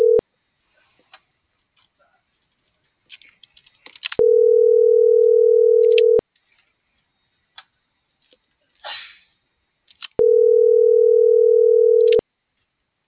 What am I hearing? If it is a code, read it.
Ringback tone (North America), 440 + 480 Hz.